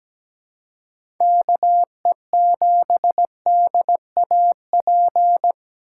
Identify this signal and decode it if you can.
Morse code, heard as a keyed tone, about 17 wpm, 700 Hz. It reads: KE7DAP
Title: USA Call Signs - 6 Characters - 3x Character Spacing - 17wpm